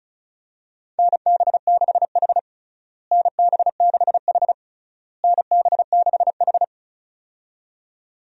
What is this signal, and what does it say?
Morse code, heard as a keyed tone, about 36 wpm, 700 Hz. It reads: NB6H NB6H NB6H
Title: USA Call Signs - 4 Characters - Repeated 3x - 35wpm